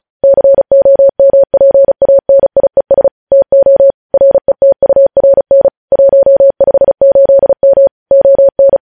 Morse code, heard as a keyed tone, about 36 wpm, 550 Hz. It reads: COMPANIES TO RETURN 158M ON